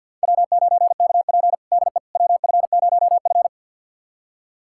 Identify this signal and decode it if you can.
Morse code, heard as a keyed tone, about 52 wpm, 700 Hz. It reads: W9KP DE WF0F
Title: USA Callsign DE Callsign - 4 Characters - 50wpm